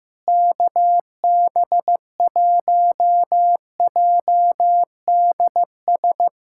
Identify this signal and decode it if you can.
Morse code, heard as a keyed tone, about 15 wpm, 700 Hz. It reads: KB1JDS
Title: USA Callsign DE Callsign 15wpm